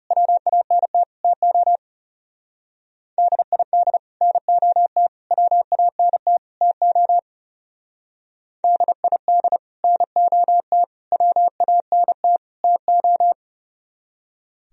Morse code, heard as a keyed tone, about 35 wpm, 700 Hz. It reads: WANT TO DID NOT WANT TO DID NOT WANT TO
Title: Speed-Racing - Sets of 4 Words 30wpm